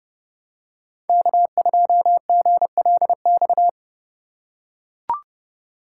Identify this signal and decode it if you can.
Morse code, heard as a keyed tone, about 32 wpm, 700 Hz. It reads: K2GLX
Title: USA Call Signs - 5 Characters - 2x Character Spacing - 30wpm